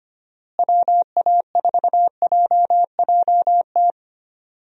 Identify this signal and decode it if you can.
Morse code, heard as a keyed tone, about 25 wpm, 700 Hz. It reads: WA4JJT
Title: USA Call Signs - 6 Characters - Repeated 2x - 25wpm